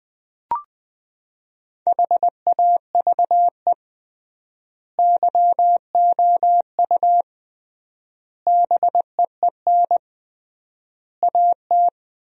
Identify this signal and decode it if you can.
Morse code, heard as a keyed tone, about 20 wpm, 700 Hz. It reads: HAVE YOU BEEN AT
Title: 3x Word-Spacing - Sentences from Top 300 Words - 20wpm